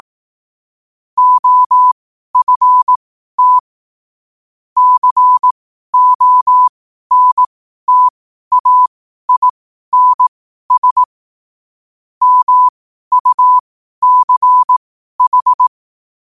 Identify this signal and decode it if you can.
Morse code, heard as a keyed tone, about 18 wpm, spaced as for 12 wpm, 1000 Hz. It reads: OFT CONTAINS MUCH
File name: qotdfiles / 12wpm